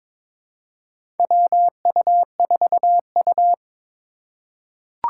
Morse code, heard as a keyed tone, about 22 wpm, 700 Hz. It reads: WU4U
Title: USA Call Signs - 3x Character Spacing - 22wpm